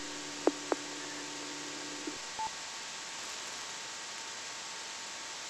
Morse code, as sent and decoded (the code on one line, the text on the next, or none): none